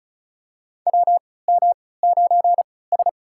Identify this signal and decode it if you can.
Morse code, heard as a keyed tone, about 34 wpm, spaced as for 20 wpm, 700 Hz. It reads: WM9S
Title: USA Call Signs - 4 Characters - 3x Character Spacing - 35wpm